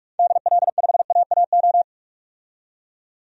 Morse code, heard as a keyed tone, about 41 wpm, 700 Hz. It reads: DL5AAO
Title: DXCC-100 Call Signs - 6 Characters - 45wpm